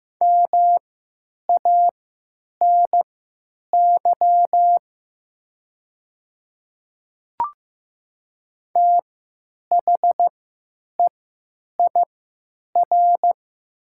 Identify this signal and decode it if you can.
Morse code, heard as a keyed tone, about 15 wpm, spaced as for 9 wpm, 700 Hz. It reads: MANY THEIR
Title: 3x Character Spacing - Top 100 Words - 15wpm